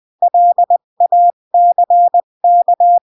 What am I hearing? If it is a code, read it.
Morse code, heard as a keyed tone, about 20 wpm, 700 Hz. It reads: LACK
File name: CW / jodorowsky / 82 commandments 20-15